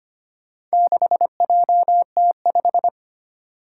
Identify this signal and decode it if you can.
Morse code, heard as a keyed tone, about 25 wpm, 700 Hz. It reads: 6JT5
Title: Four Challenging Letter-Number Combinations 25wpm